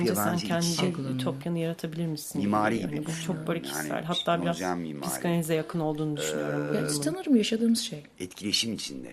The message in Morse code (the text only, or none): none